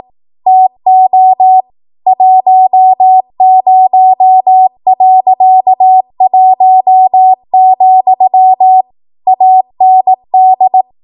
Morse code, heard as a keyed tone, about 18 wpm, 750 Hz. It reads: TO 10.1, AND
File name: Archive / 18 WPM / 18WPM